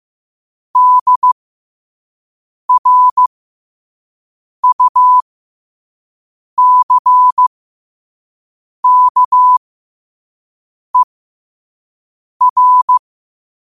Morse code, heard as a keyed tone, about 15 wpm, spaced as for 5 wpm, 1000 Hz. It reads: DRUCKER
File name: qotdfiles / 5wpm